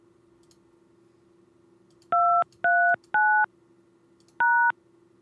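DTMF, heard as telephone touch tones, tones of about 330 ms. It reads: 239 #
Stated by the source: Prank phone calls made on a show streamed live on YouTube.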